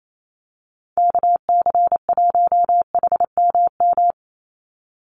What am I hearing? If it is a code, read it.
Morse code, heard as a keyed tone, about 28 wpm, 700 Hz. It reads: KC1HMM
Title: USA Call Signs 28wpm